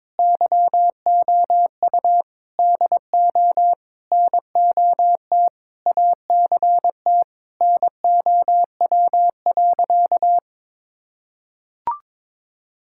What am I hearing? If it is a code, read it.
Morse code, heard as a keyed tone, about 22 wpm, 700 Hz. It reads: YOU DO NOT ACT NOW.
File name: practice / 22wpm / Sentences with 8-Word Phrases - 22wpm